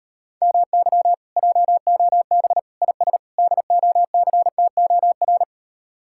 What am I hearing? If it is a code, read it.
Morse code, heard as a keyed tone, about 38 wpm, 700 Hz. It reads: MY JOB IS DOCTOR